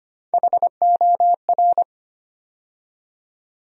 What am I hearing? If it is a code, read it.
Morse code, heard as a keyed tone, about 25 wpm, 700 Hz. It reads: HOR
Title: USA Call Sign Postfixes - 3 Letters 25wpm